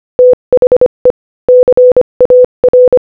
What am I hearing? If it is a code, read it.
Morse code, heard as a keyed tone, about 25 wpm, 500 Hz. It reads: THE CAR
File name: Morse-Code-Audio-Speed-25-2